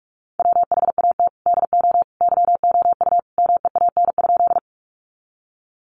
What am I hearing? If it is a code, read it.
Morse code, heard as a keyed tone, about 44 wpm, 700 Hz. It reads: WHAT DO YOU MEAN?
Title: Sentences from Top 200 Words 45wpm